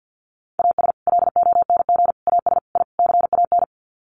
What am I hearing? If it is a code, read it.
Morse code, heard as a keyed tone, about 48 wpm, 700 Hz. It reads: AS LONG AS I CAN
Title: Sets of 5 Words Encore2 50wpm